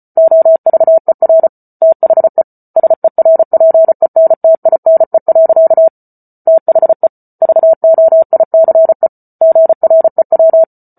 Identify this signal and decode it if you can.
Morse code, heard as a keyed tone, about 34 wpm, 650 Hz. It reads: OVER THE SERPENTINE. THE VOICE GREW
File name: Book Two / 08-Chapter 8 - 34 WPM